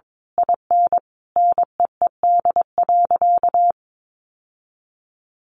Morse code, heard as a keyed tone, about 22 wpm, 700 Hz. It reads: IN NEED.
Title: Sentences from Top 1200 Words 22wpm